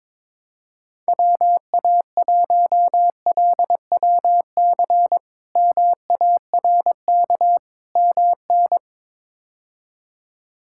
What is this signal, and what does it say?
Morse code, heard as a keyed tone, about 22 wpm, 700 Hz. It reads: WA1LWC MARK MN
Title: K1USN SST QSO Practice - 22wpm